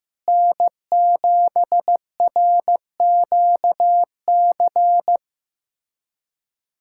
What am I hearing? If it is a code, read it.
Morse code, heard as a keyed tone, about 15 wpm, 700 Hz. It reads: N7RQC
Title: USA Callsign DE Callsign - 5 Characters - 15wpm